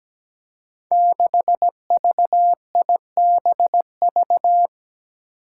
Morse code, heard as a keyed tone, about 17 wpm, 700 Hz. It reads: 6VIBV
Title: Five Challenging Letter-Number Combinations 17wpm